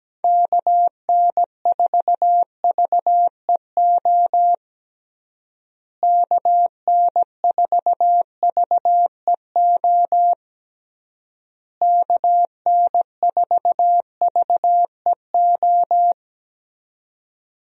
Morse code, heard as a keyed tone, about 17 wpm, 700 Hz. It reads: KN4VEO KN4VEO KN4VEO
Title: USA Call Signs - 6 Characters - Repeated 3x - 17wpm